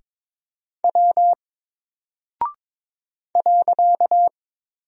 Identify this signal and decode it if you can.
Morse code, heard as a keyed tone, about 22 wpm, 700 Hz. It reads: W .
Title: Single Character (Letter:Number:Punctuation:Prosign) Alt 22wpm